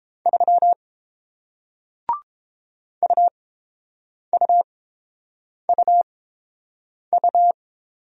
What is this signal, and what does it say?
Morse code, heard as a keyed tone, about 31 wpm, spaced as for 7 wpm, 700 Hz. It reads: 3 UUUU